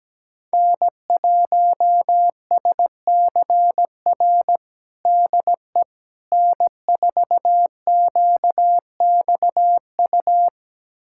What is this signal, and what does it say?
Morse code, heard as a keyed tone, about 17 wpm, 700 Hz. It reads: N1SCR DE N4QXU
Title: USA Callsign DE Callsign - 5 Characters - 17wpm